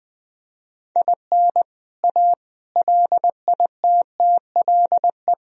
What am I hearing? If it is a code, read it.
Morse code, heard as a keyed tone, about 20 wpm, 700 Hz. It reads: IN A LITTLE
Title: Sentences from Top 1000 Words - Review Part 1 - 20wpm